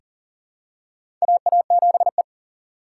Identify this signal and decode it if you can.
Morse code, heard as a keyed tone, about 38 wpm, 700 Hz. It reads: AA7E
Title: USA Call Signs - 4 Characters - 6x Character Spacing - 40wpm